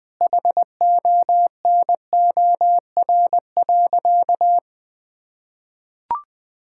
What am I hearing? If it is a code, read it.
Morse code, heard as a keyed tone, about 20 wpm, 700 Hz. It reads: HONOR.
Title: Sentences with 5-Word Phrases - 20wpm